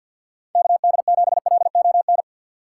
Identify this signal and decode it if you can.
Morse code, heard as a keyed tone, about 51 wpm, 700 Hz. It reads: KD7LON
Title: USA Call Signs - 6 Characters - 3x Character Spacing - 50wpm